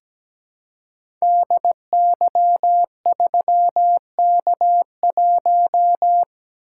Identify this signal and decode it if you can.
Morse code, heard as a keyed tone, about 17 wpm, 700 Hz. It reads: DY3K1
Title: Code Groups - 5 Letter:Numbers - 17wpm